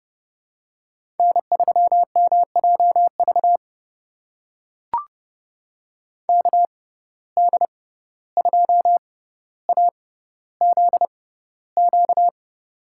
Morse code, heard as a keyed tone, about 30 wpm, spaced as for 15 wpm, 700 Hz. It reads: N3MJV KD2AZQ
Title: USA Call Signs - 6x Character Spacing - 30wpm